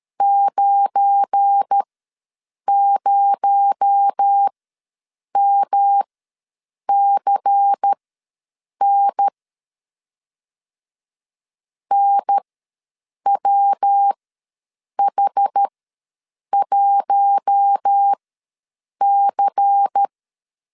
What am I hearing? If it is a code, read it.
Morse code, heard as a keyed tone, about 13 wpm, spaced as for 7 wpm, 800 Hz. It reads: 90MCN NWH1C
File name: LEZIONI / 8 YJ45 / 8 YJ4 ws 12wpm 3249